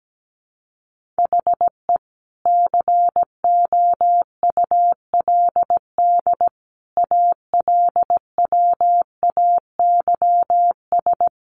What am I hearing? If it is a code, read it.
Morse code, heard as a keyed tone, about 17 wpm, 700 Hz. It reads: HE COULD ALWAYS
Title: Sentences from Top 1400 Words 17wpm